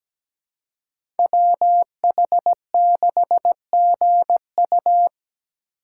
Morse code, heard as a keyed tone, about 17 wpm, 700 Hz. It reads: WH6GU
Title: USA Call Signs - 5 Characters - 6x Character Spacing - 17wpm